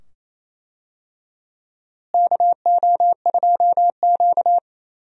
Morse code, heard as a keyed tone, about 28 wpm, 700 Hz. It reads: KO2Q
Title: USA Call Signs - 4 Characters - Repeated 2x - 28wpm